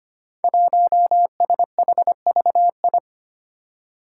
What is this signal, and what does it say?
Morse code, heard as a keyed tone, about 25 wpm, 700 Hz. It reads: 1SHVI
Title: Five Challenging Letter-Number Combinations 25wpm